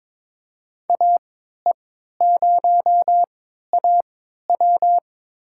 Morse code, heard as a keyed tone, about 21 wpm, spaced as for 12 wpm, 700 Hz. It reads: AE0AW
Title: USA Call Signs - 5 Characters - 3x Character Spacing - 22wpm